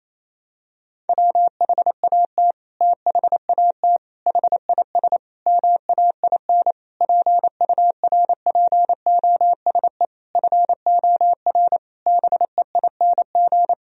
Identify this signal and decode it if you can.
Morse code, heard as a keyed tone, about 28 wpm, 700 Hz. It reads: WHAT THAT HIS MAIN PURPOSE FOR BEING
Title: Sentences from Top 1000 Words - Review Part 2 - 28wpm